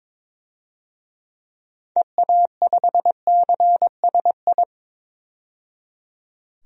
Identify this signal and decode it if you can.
Morse code, heard as a keyed tone, about 22 wpm, 700 Hz. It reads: EA5CSI
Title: DXCC-100 Call Signs - 6 Characters - 22wpm